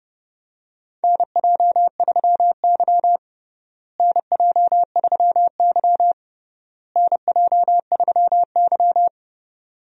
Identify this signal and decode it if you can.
Morse code, heard as a keyed tone, about 32 wpm, 700 Hz. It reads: NJ3Y NJ3Y NJ3Y
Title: USA Call Signs - 4 Characters - Repeated 3x - 30wpm